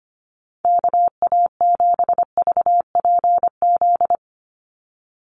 Morse code, heard as a keyed tone, about 25 wpm, 700 Hz. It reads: KA7VPZ
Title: USA Call Signs 25wpm